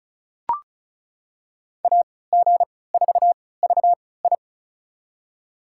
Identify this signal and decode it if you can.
Morse code, heard as a keyed tone, about 36 wpm, spaced as for 20 wpm, 700 Hz. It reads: AG4VI
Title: USA Call Signs - 5 Characters - 3x Character Spacing - 35wpm